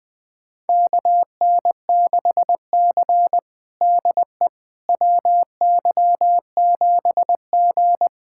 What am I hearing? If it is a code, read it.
Morse code, heard as a keyed tone, about 20 wpm, 700 Hz. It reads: KN6C DE WY7G
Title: USA Callsign DE Callsign - 4 Characters - 20wpm